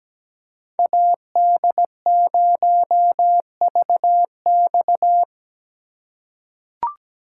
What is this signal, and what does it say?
Morse code, heard as a keyed tone, about 17 wpm, 700 Hz. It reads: AD0VX
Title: USA Call Signs - 5 Characters - 3x Character Spacing - 17wpm